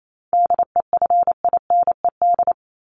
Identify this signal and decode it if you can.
Morse code, heard as a keyed tone, about 28 wpm, 700 Hz. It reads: DEFINED
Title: ED Suffix - Top 100 Words with Base Words 28wpm